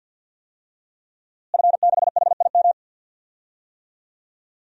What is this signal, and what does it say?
Morse code, heard as a keyed tone, about 47 wpm, 700 Hz. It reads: F6HIM
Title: DXCC-100 Call Signs - 5 Characters - Repeated 2x - 50wpm